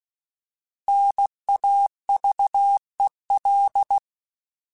Morse code, heard as a keyed tone, about 16 wpm, 800 Hz. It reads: NAVEL